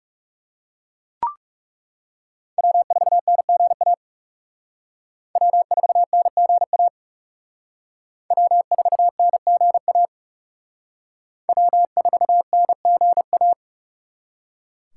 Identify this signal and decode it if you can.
Morse code, heard as a keyed tone, about 37 wpm, 700 Hz. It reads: W4NGA W4NGA W4NGA W4NGA
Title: Speed Racing - USA 1x3 Call Signs 30wpm